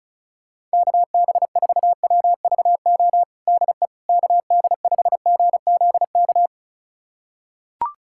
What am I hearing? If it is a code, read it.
Morse code, heard as a keyed tone, about 34 wpm, 700 Hz. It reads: KB4WVO DE KD5GZK
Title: USA Callsign DE Callsign 35wpm